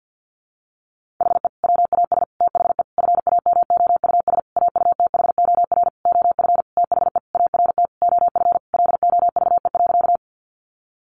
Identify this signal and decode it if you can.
Morse code, heard as a keyed tone, about 50 wpm, 700 Hz. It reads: HE WAS THE FAMOUS AUTHOR OF THE ART OF LOVE.
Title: Sentences from Top 1500 Words 50wpm